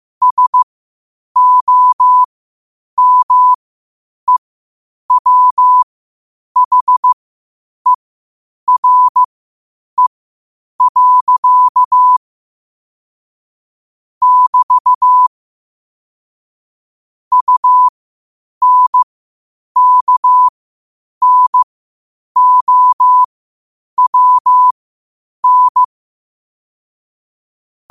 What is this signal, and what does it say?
Morse code, heard as a keyed tone, about 15 wpm, spaced as for 8 wpm, 1000 Hz. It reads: SOMEWHERE. <BT> UNKNOWN
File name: qotdfiles / 8wpm